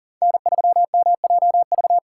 Morse code, heard as a keyed tone, about 40 wpm, 700 Hz. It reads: N3MJV
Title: USA Call Signs - 5 Characters - 4x Character Spacing - 40wpm